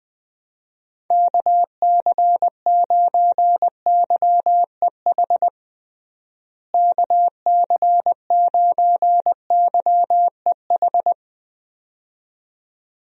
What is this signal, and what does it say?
Morse code, heard as a keyed tone, about 20 wpm, 700 Hz. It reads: KC9YEH KC9YEH
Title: USA Call Signs - 6 Characters - Repeated 2x - 20wpm